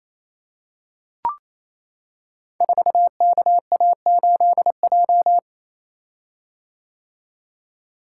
Morse code, heard as a keyed tone, about 28 wpm, 700 Hz. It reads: 4KA8J
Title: Code Groups - 5 Letter:Numbers - 28wpm